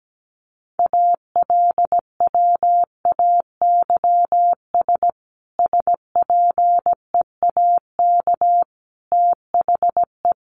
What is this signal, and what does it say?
Morse code, heard as a keyed tone, about 17 wpm, 700 Hz. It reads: ALWAYS SPEAK THE
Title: Sentences from Top 1100 Words 17wpm